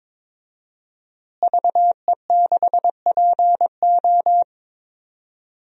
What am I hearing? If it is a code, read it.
Morse code, heard as a keyed tone, about 21 wpm, 700 Hz. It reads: VE6PO
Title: DXCC-100 Call Signs - 5 Characters - Repeated 3x - 22wpm